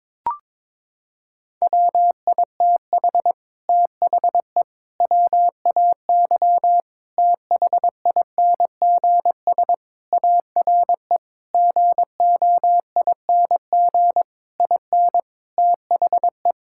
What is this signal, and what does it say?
Morse code, heard as a keyed tone, about 22 wpm, 700 Hz. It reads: WITH THE WAY THINGS ARE GOING IN THE
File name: practice / 22wpm / Sentences with 8-Word Phrases - 22wpm